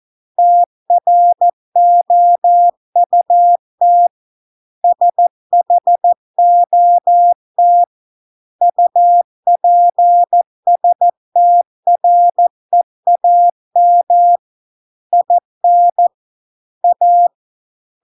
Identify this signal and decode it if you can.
Morse code, heard as a keyed tone, about 14 wpm, 700 Hz. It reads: TROUT SHOT UPSTREAM IN A